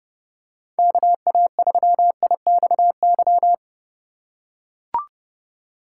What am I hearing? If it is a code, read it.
Morse code, heard as a keyed tone, about 30 wpm, 700 Hz. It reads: KA3IXY